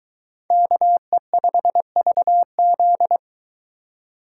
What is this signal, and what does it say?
Morse code, heard as a keyed tone, about 22 wpm, 700 Hz. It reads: KE5VZ